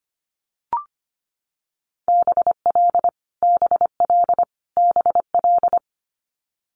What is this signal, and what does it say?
Morse code, heard as a keyed tone, about 25 wpm, 700 Hz. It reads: BL BL BL
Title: Multi-letter Phonemes Repeated 3x 25wpm